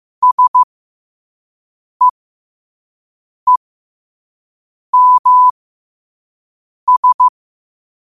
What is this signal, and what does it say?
Morse code, heard as a keyed tone, about 15 wpm, spaced as for 5 wpm, 1000 Hz. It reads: SEEMS
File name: 5wpm